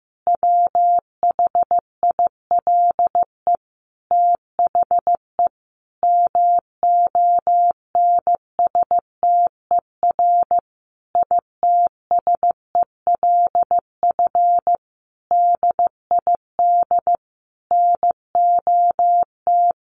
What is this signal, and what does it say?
Morse code, heard as a keyed tone, about 15 wpm, 700 Hz. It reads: WHILE THE MONSTER ITSELF DID NOT
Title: A Sound Of Thunder 15wpm